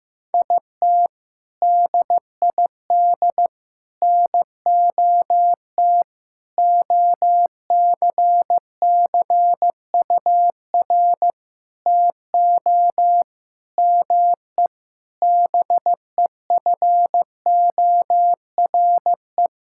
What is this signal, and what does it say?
Morse code, heard as a keyed tone, about 15 wpm, 700 Hz. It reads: IT DID NOT OCCUR TO ME BEFORE
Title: Sentences from Top 1000 Words - Review Part 6 - 15wpm